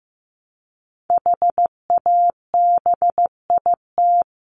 Morse code, heard as a keyed tone, about 15 wpm, 700 Hz. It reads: HABIT
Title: Top 1751-2000 Words 15wpm